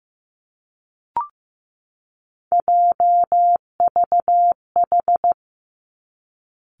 Morse code, heard as a keyed tone, about 15 wpm, 700 Hz. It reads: JVH
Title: Three Challenging Letter-Number Combinations 15wpm